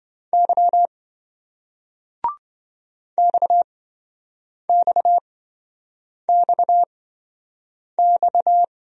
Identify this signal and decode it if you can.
Morse code, heard as a keyed tone, about 26 wpm, spaced as for 7 wpm, 700 Hz. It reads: Y XXXX